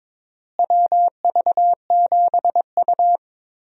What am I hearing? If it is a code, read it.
Morse code, heard as a keyed tone, about 22 wpm, 700 Hz. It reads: WV7U
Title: USA Call Signs - 4 Characters - Repeated 3x - 22wpm